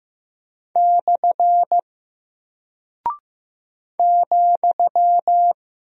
Morse code, heard as a keyed tone, about 15 wpm, 700 Hz. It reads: / ,
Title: Single Character (Letter:Number:Punctuation:Prosign) Alt 15wpm